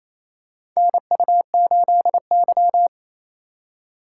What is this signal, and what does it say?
Morse code, heard as a keyed tone, about 28 wpm, 700 Hz. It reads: NU8Y
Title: USA Call Signs - 4 Characters - 4x Character Spacing - 28wpm